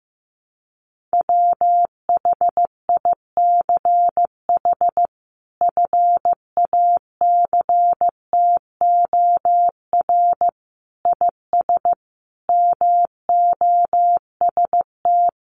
Morse code, heard as a keyed tone, about 15 wpm, 700 Hz. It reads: WHICH FACTOR IS MOST
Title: Sentences from Top 1300 Words 15wpm